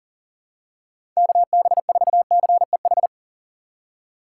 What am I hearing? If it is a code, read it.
Morse code, heard as a keyed tone, about 37 wpm, 700 Hz. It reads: KB4CEH
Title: USA Call Signs - 6 Characters - 4x Character Spacing - 40wpm